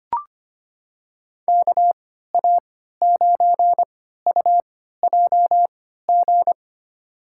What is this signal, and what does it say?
Morse code, heard as a keyed tone, about 25 wpm, spaced as for 14 wpm, 700 Hz. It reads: KA9UJG
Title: USA Call Signs - 6 Characters - 3x Character Spacing - 25wpm